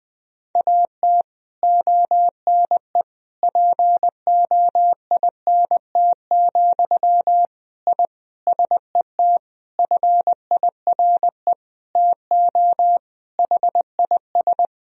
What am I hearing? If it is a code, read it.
Morse code, heard as a keyed tone, about 20 wpm, 700 Hz. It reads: AT ONE POINT, I SET FIRE TO HIS
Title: Sentences from Top 1000 Words - Review Part 1 - 20wpm